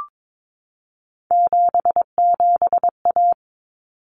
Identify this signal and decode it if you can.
Morse code, heard as a keyed tone, about 22 wpm, 700 Hz. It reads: 77A
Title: Common Cut Numbers - 3-Digit 22wpm